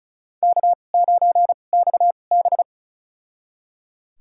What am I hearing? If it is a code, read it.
Morse code, heard as a keyed tone, about 36 wpm, spaced as for 25 wpm, 700 Hz. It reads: K9XB
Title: USA Call Signs - 4 Characters - 2x Character Spacing - 35wpm